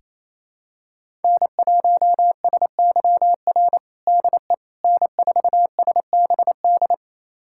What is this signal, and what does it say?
Morse code, heard as a keyed tone, about 28 wpm, 700 Hz. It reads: N1SYR DE N4SBD
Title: USA Callsign DE Callsign - 5 Characters - 28wpm